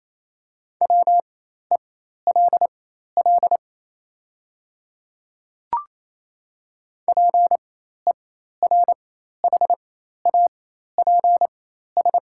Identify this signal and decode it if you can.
Morse code, heard as a keyed tone, about 28 wpm, spaced as for 13 wpm, 700 Hz. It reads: WELL PERHAPS
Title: 4x Character Spacing - Top 500 Words - 28wpm